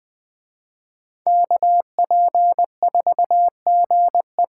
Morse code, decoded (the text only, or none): KP4GE